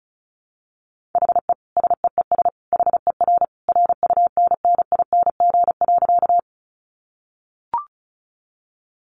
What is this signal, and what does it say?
Morse code, heard as a keyed tone, about 34 wpm, 700 Hz. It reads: HE SEES HER RUNNING.
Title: Additional Easy Phrases 35wpm